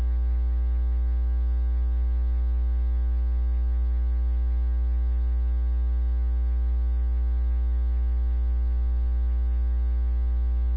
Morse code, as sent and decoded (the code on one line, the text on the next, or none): none